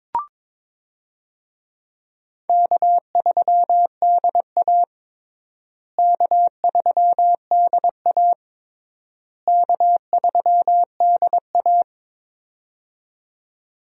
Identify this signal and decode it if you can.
Morse code, heard as a keyed tone, about 22 wpm, 700 Hz. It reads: K3DA K3DA K3DA
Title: USA Call Signs - 4 Characters - Repeated 3x - 22wpm